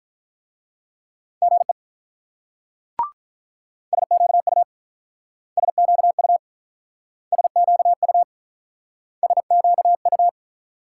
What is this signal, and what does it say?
Morse code, heard as a keyed tone, about 44 wpm, 700 Hz. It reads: GE SQU SQU SQU SQU